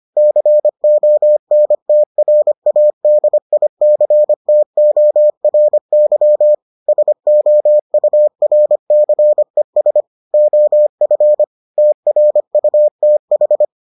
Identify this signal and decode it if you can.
Morse code, heard as a keyed tone, about 25 wpm, 600 Hz. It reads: CONTRADICTORY SOURCES OF TRUTH